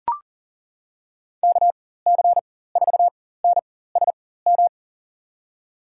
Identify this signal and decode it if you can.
Morse code, heard as a keyed tone, about 40 wpm, spaced as for 19 wpm, 700 Hz. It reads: KC4NSM
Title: USA Call Signs - 6 Characters - 4x Character Spacing - 40wpm